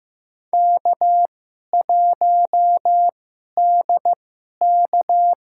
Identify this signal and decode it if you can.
Morse code, heard as a keyed tone, about 15 wpm, spaced as for 11 wpm, 700 Hz. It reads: K1DK
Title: USA Call Signs - 4 Characters - 2x Character Spacing - 15wpm